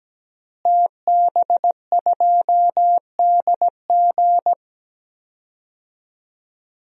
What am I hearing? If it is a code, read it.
Morse code, heard as a keyed tone, about 17 wpm, 700 Hz. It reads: TB2DG
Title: DXCC-100 Call Signs - 5 Characters - Repeated 2x - 17wpm